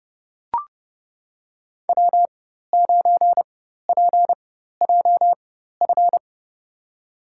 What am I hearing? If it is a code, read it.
Morse code, heard as a keyed tone, about 30 wpm, spaced as for 14 wpm, 700 Hz. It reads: W9PJF
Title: USA Call Signs - 5 Characters - 4x Character Spacing - 30wpm